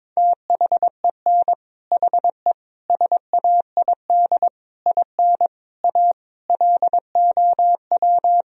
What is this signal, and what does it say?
Morse code, heard as a keyed tone, about 22 wpm, 700 Hz. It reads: THEN HE SAID IN A LOW
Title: Sentences from Top 1000 Words - Review Part 1 - 22wpm